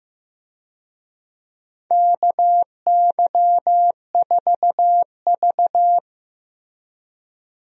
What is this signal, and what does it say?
Morse code, heard as a keyed tone, about 15 wpm, 700 Hz. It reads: KY4V
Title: USA Call Signs - 4 Characters - Repeated 2x - 15wpm